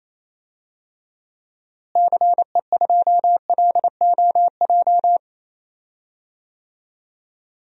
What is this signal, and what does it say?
Morse code, heard as a keyed tone, about 28 wpm, 700 Hz. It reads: CE2LOJ
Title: DXCC-100 Call Signs - 6 Characters - Repeated 3x - 28wpm